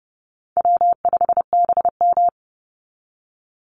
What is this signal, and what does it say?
Morse code, heard as a keyed tone, about 29 wpm, 700 Hz. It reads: W5BM
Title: Easy Call Signs Repeated 2x - 30wpm